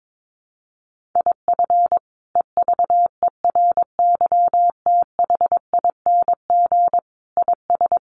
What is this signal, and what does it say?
Morse code, heard as a keyed tone, about 22 wpm, 700 Hz. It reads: IF EVERYTHING IS
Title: Sentences from Top 700 Words 22wpm